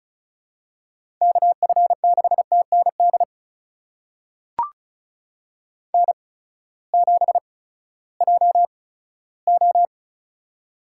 Morse code, heard as a keyed tone, about 36 wpm, spaced as for 18 wpm, 700 Hz. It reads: KF6TND N7JO
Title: USA Call Signs - 8x Character Spacing - 35wpm